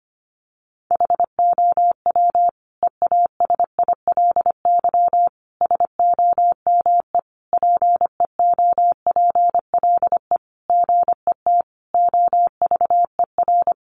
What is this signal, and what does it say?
Morse code, heard as a keyed tone, about 25 wpm, 700 Hz. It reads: HOW EASILY SOME PEOPLE GET OVER
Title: Sentences from Top 900 Words 25wpm